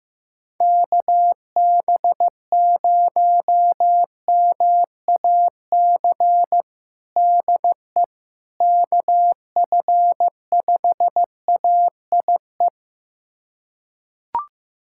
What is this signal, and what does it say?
Morse code, heard as a keyed tone, about 16 wpm, 700 Hz. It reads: KB0MAC DE KF5AIE
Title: USA Callsign DE Callsign - 6 Characters - 15wpm